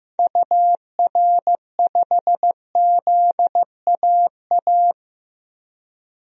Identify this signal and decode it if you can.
Morse code, heard as a keyed tone, about 15 wpm, 700 Hz. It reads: UR5ZAA
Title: DXCC-100 Call Signs - 6 Characters - 15wpm